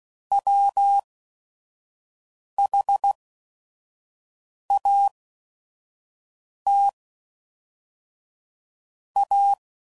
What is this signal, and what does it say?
Morse code, heard as a keyed tone, about 16 wpm, spaced as for 5 wpm, 800 Hz. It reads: WHAT A
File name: Sayings / Quotes / Deceive